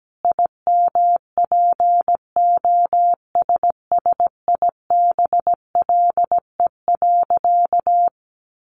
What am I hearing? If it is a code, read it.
Morse code, heard as a keyed tone, about 17 wpm, 700 Hz. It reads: IMPOSSIBLE.
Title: Sentences from Top 1100 Words 17wpm